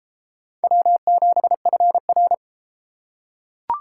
Morse code, heard as a keyed tone, about 33 wpm, 700 Hz. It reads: W7FR